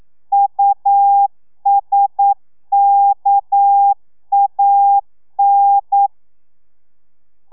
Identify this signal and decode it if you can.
Morse code, heard as a keyed tone, about 9 wpm, 800 Hz. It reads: USKAN
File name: telegrafia / i2tfj / Lez6 40